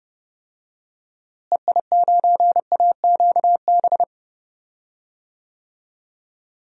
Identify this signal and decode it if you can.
Morse code, heard as a keyed tone, about 30 wpm, 700 Hz. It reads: EI9AQB
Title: DXCC-100 Call Signs - 6 Characters - Repeated 2x - 30wpm